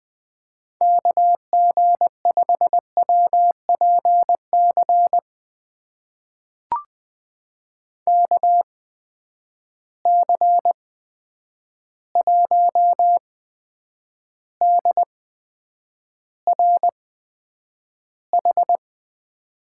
Morse code, heard as a keyed tone, about 20 wpm, spaced as for 9 wpm, 700 Hz. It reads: KG5WPC KC1DRH
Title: USA Call Signs - 6 Characters - 8x Character Spacing - 20wpm